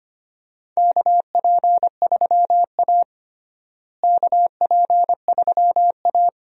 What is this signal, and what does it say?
Morse code, heard as a keyed tone, about 25 wpm, 700 Hz. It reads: KP3A KP3A
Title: USA Call Signs - 4 Characters - Repeated 2x - 25wpm